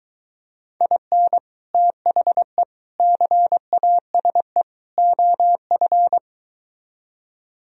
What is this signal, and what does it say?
Morse code, heard as a keyed tone, about 24 wpm, 700 Hz. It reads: IN THE CASE OF